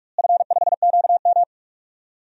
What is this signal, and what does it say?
Morse code, heard as a keyed tone, about 45 wpm, 700 Hz. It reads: F5QM
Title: DXCC-100 Call Signs - 4 Characters - Repeated 3x - 45wpm